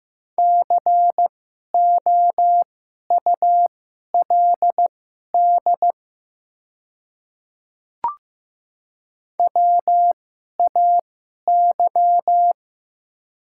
Morse code, heard as a keyed tone, about 15 wpm, spaced as for 11 wpm, 700 Hz. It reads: COULD WAY
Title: 2x Character Spacing - Top 100 Words - 15wpm